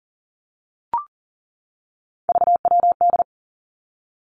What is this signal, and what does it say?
Morse code, heard as a keyed tone, about 38 wpm, 700 Hz. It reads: VWD
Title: Three Challenging Letter-Number Combinations 40wpm